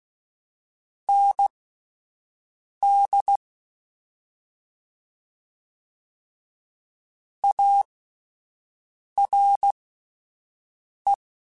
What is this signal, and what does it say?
Morse code, heard as a keyed tone, about 16 wpm, spaced as for 5 wpm, 800 Hz. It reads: ND ARE
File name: (spaced) / QSOs Spaced-mp3s / WD0BGP 1 @10 sp